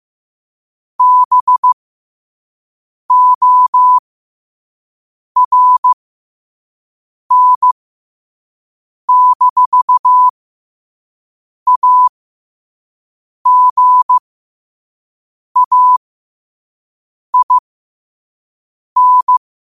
Morse code, heard as a keyed tone, about 15 wpm, spaced as for 5 wpm, 1000 Hz. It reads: BORN-AGAIN
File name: qotdfiles / 5wpm